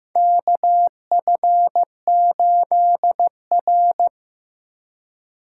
Morse code, heard as a keyed tone, about 15 wpm, 700 Hz. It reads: KF8R
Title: USA Call Signs - 3x Character Spacing - 15wpm